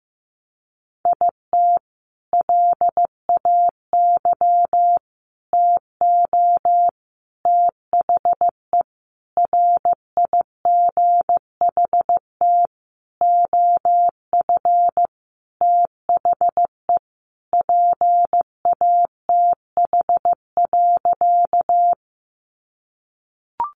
Morse code, heard as a keyed tone, about 15 wpm, 700 Hz. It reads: IT LAY TO THE RIGHT OF THE PATH.
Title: Sentences from Top 1400 Words 15wpm